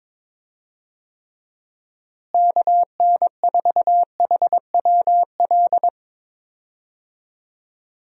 Morse code, heard as a keyed tone, about 22 wpm, 700 Hz. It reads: KN4HWL
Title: USA Call Signs - 6 Characters - Repeated 3x - 22wpm